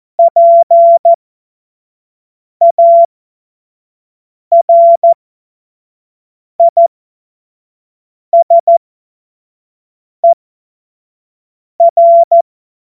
Morse code, heard as a keyed tone, about 14 wpm, spaced as for 5 wpm, 680 Hz. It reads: PARISER
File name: Morsekurs / Witze / Witz 02. Mai 2023 25 F70